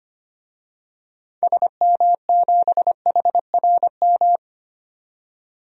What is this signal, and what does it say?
Morse code, heard as a keyed tone, about 25 wpm, 700 Hz. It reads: SM7HRM
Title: DXCC-100 Call Signs - 6 Characters - 25wpm